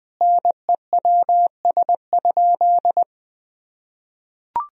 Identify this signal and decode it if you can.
Morse code, heard as a keyed tone, about 19 wpm, 700 Hz. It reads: NEWS?
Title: Sentences from Top 1000 Words - Review Part 4 - 20wpm